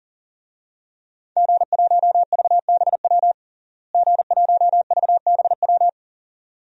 Morse code, heard as a keyed tone, about 40 wpm, 700 Hz. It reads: G1VBW G1VBW
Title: DXCC-100 Call Signs - 5 Characters - Repeated 2x - 40wpm